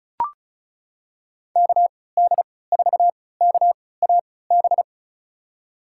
Morse code, heard as a keyed tone, about 34 wpm, spaced as for 20 wpm, 700 Hz. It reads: KD4KAB
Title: USA Call Signs - 6 Characters - 3x Character Spacing - 35wpm